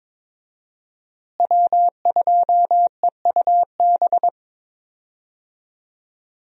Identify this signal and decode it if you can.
Morse code, heard as a keyed tone, about 22 wpm, 700 Hz. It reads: W2EUB
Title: USA Call Signs - 5 Characters - Repeated 3x - 22wpm